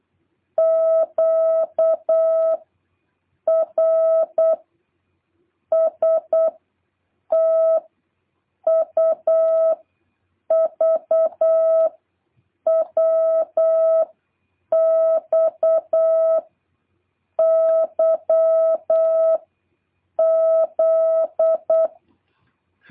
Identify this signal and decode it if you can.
Morse code, heard as a keyed tone, about 8 wpm, 650 Hz. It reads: Q R S T U V W X Y Z